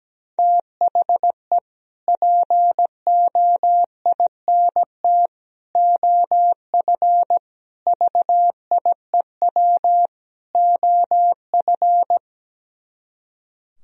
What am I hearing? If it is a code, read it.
Morse code, heard as a keyed tone, about 17 wpm, 700 Hz. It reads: THE POINT OF VIEW OF
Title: Speed-Racing - Sets of 5 Words 17wpm